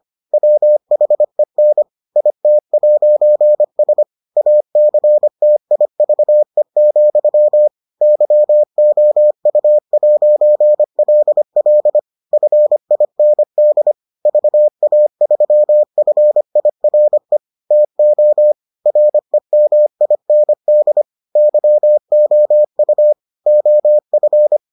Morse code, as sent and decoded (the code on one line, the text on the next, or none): .-- .... -. / .. - .----. ... / .- -.-. - .. ...- . --..-- / -.-- --- ..- .----. .-.. .-.. / ..-. .. -. -.. / ...- .- ...-- ..-. .. .-. / - --- / .-. -- .. -. -.. / -.-- --- ..- / --- ..-.
WHEN IT'S ACTIVE, YOU'LL FIND VA3FIRE TO REMIND YOU OF